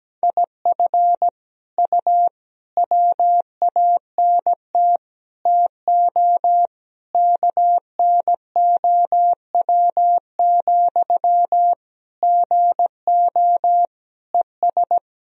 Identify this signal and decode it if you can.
Morse code, heard as a keyed tone, about 17 wpm, 700 Hz. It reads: IF U WANT TO KNOW, GO ES
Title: Sentences from Top 200 Words and Abbreviations 17wpm